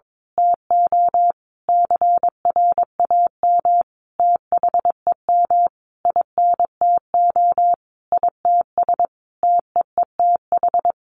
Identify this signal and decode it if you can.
Morse code, heard as a keyed tone, about 22 wpm, 700 Hz. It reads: TO CRAM THEM INTO ITS TEETH
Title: A Sound Of Thunder 22wpm